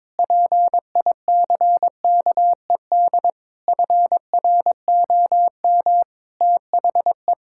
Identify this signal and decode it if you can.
Morse code, heard as a keyed tone, about 22 wpm, 700 Hz. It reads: PICKED FROM THE
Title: Pangram Challenge v2 - 22wpm